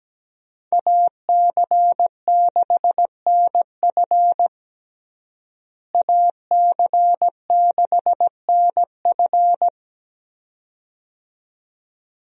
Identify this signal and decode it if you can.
Morse code, heard as a keyed tone, about 17 wpm, 700 Hz. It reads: AC6NF AC6NF
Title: USA Call Signs - 5 Characters - Repeated 2x - 17wpm